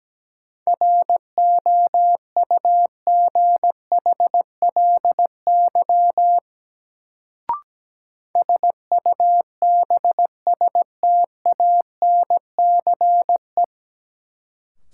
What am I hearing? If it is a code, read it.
Morse code, heard as a keyed tone, about 17 wpm, 700 Hz. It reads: ROUGHLY SUBSTANCE
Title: Sentences from Top 1500 Words - New Word Drill - 17wpm